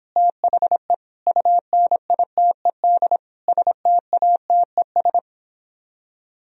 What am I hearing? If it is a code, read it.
Morse code, heard as a keyed tone, about 26 wpm, 700 Hz. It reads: THE UNITED STATES